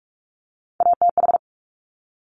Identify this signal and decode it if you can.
Morse code, heard as a keyed tone, about 46 wpm, 700 Hz. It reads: ATH
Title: Top Three Letter Combinations in Words 45wpm